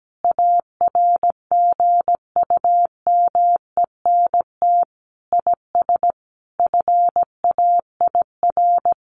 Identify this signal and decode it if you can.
Morse code, heard as a keyed tone, about 17 wpm, 700 Hz. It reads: ARGUMENT IS FAIR
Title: Sentences from Top 1400 Words 17wpm